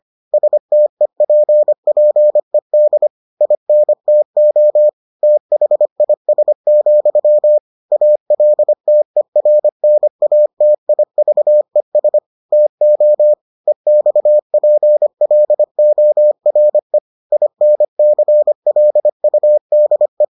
Morse code, heard as a keyed tone, about 25 wpm, 600 Hz. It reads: STEPPED INTO THIS, ALTERNATIVES TO EXPLORE INCLUDE